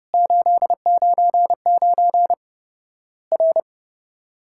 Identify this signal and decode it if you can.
Morse code, heard as a keyed tone, about 30 wpm, 700 Hz. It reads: 899 R
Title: Worked All Europe DX Contest - QTC Exchange - 30wpm